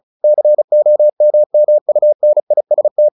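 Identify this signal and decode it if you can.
Morse code, heard as a keyed tone, about 34 wpm, 600 Hz. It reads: COMMUNIST
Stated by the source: Morse practice at 35 wpm from CW med Gnister, sending Ken Follett's novel Never in English.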